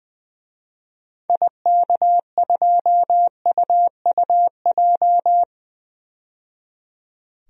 Morse code, heard as a keyed tone, about 21 wpm, 700 Hz. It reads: IK2UUJ
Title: DXCC-100 Call Signs - 6 Characters - 20wpm